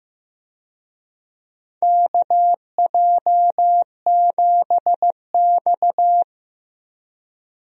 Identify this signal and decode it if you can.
Morse code, heard as a keyed tone, about 15 wpm, 700 Hz. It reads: KJ7X
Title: USA Call Signs - 4 Characters - Repeated 3x - 15wpm